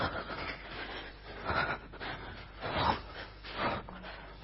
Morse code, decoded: none

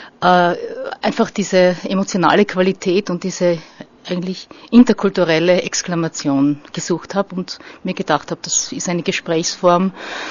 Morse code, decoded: none

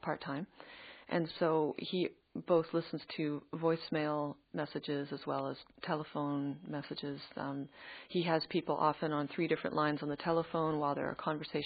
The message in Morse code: none

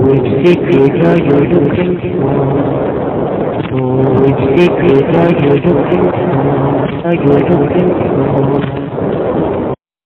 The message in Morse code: none

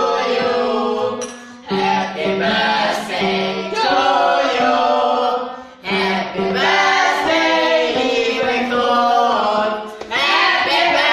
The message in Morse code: none

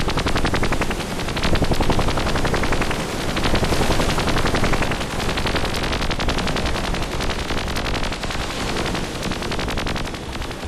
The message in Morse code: none

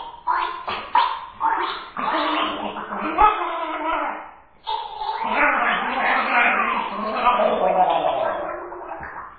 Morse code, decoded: none

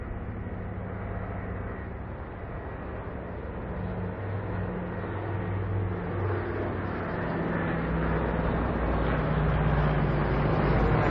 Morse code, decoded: none

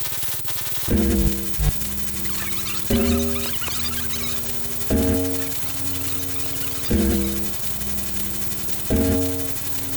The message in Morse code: none

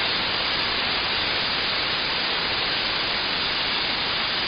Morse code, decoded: none

